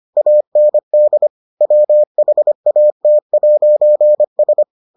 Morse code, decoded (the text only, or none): AND WHAT'S